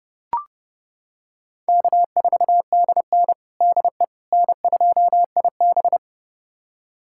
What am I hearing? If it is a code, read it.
Morse code, heard as a keyed tone, about 30 wpm, 700 Hz. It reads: K4DN DE N2IB
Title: USA Callsign DE Callsign - 4 Characters - 30wpm